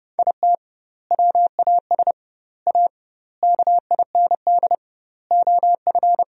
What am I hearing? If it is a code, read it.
Morse code, heard as a keyed tone, about 30 wpm, 700 Hz. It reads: IT WAS A KIND OF
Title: 2x Word-Spacing - Sentences from Top 500 Words - 30wpm